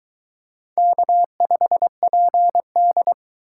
Morse code, heard as a keyed tone, about 23 wpm, 700 Hz. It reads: K5PD